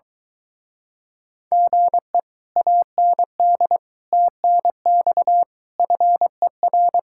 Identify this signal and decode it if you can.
Morse code, heard as a keyed tone, about 24 wpm, 700 Hz. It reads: GE AND TNX FER